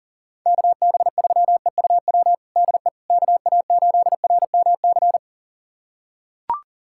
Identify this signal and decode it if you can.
Morse code, heard as a keyed tone, about 40 wpm, 700 Hz. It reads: KB3EUW DE KA8RMC
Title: USA Callsign DE Callsign - 6 Characters - 40wpm